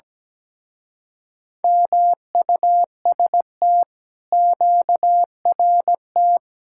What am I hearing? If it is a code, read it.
Morse code, heard as a keyed tone, about 17 wpm, 700 Hz. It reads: MUST QRT
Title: Speed-Racing Common QSO Phrases - 17wpm